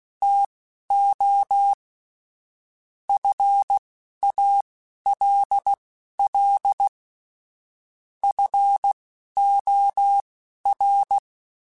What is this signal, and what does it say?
Morse code, heard as a keyed tone, about 16 wpm, spaced as for 11 wpm, 800 Hz. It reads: TO FALL FOR
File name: Quotes / Quotes-mp3s / StandFor @10